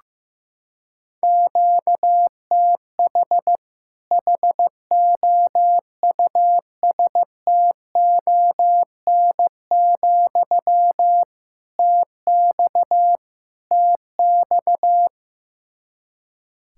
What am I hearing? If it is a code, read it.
Morse code, heard as a keyed tone, about 15 wpm, 700 Hz. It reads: QTH HOUSTON, TX TX
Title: Practice QSO - Exchange 1 - Beginning - 15wpm